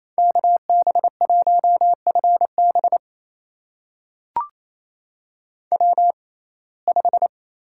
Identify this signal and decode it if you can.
Morse code, heard as a keyed tone, about 28 wpm, spaced as for 20 wpm, 700 Hz. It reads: KB1FB W5
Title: USA Call Signs - 5 Characters - 6x Character Spacing - 28wpm